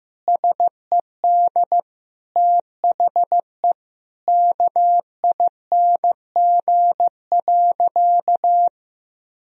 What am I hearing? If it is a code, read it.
Morse code, heard as a keyed tone, about 15 wpm, 700 Hz. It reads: SED THE KING.
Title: Sentences from Top 400 Words and Abbreviations 15wpm